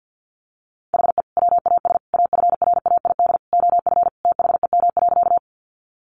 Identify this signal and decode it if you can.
Morse code, heard as a keyed tone, about 50 wpm, 700 Hz. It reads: HE WAS AFRAID OF THEM.
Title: Sentences from Top 1500 Words 50wpm